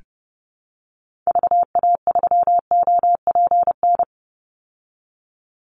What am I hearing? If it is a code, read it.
Morse code, heard as a keyed tone, about 30 wpm, 700 Hz. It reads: VA3OPN
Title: Call Signs Repeated 3x - 30wpm